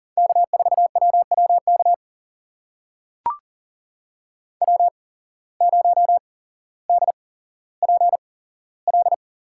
Morse code, heard as a keyed tone, about 39 wpm, spaced as for 17 wpm, 700 Hz. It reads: K4WWK W0DPL
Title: USA Call Signs - 5 Characters - 8x Character Spacing - 40wpm